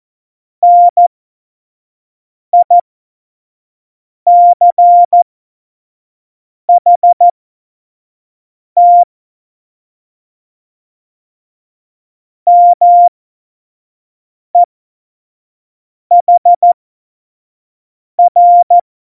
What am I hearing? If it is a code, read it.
Morse code, heard as a keyed tone, about 14 wpm, spaced as for 5 wpm, 700 Hz. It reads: NICHT MEHR
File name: Morsekurs / Witze / Witz 29. Dezember 2020, 25 F70